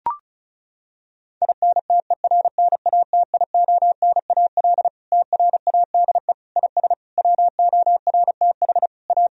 Morse code, heard as a keyed tone, about 36 wpm, 700 Hz. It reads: INTERNATIONAL TRADE IS WORTH A
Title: Sentences from Top 1000 Words - Review Part 4 - 35wpm